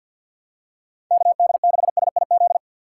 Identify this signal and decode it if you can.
Morse code, heard as a keyed tone, about 52 wpm, 700 Hz. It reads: KD6SIZ
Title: USA Call Signs - 6 Characters - 3x Character Spacing - 50wpm